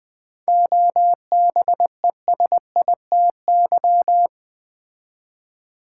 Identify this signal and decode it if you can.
Morse code, heard as a keyed tone, about 20 wpm, 700 Hz. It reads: OBESITY
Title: 7 Letter Words - First 3 Letters Given - 20wpm